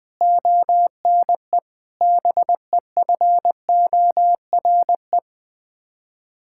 Morse code, heard as a keyed tone, about 20 wpm, 700 Hz. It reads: ONE BEFORE
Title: Sets of 8 Words - 20wpm